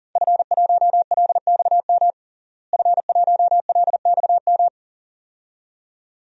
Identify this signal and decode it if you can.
Morse code, heard as a keyed tone, about 41 wpm, 700 Hz. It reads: F1LXM F1LXM
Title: DXCC-100 Call Signs - 5 Characters - Repeated 2x - 40wpm